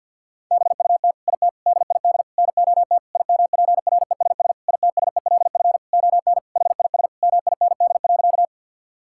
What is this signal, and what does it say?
Morse code, heard as a keyed tone, about 52 wpm, 700 Hz. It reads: BUT IT DID NOT IMPRESS ITSELF ON HIS MIND.
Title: Sentences from Top 1600 Words 50wpm